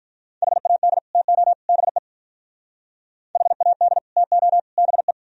Morse code, heard as a keyed tone, about 44 wpm, 700 Hz. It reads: HAD TO BE HAD TO BE